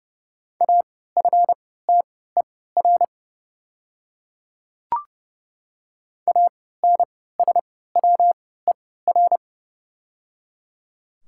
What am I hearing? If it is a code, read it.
Morse code, heard as a keyed tone, about 30 wpm, spaced as for 17 wpm, 700 Hz. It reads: AFTER ANSWER
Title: 3x Character Spacing - Top 500 Words - 30wpm